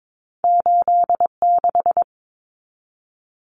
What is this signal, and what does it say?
Morse code, heard as a keyed tone, about 22 wpm, 700 Hz. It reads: 86
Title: Common Cut Numbers - 2-Digit 22wpm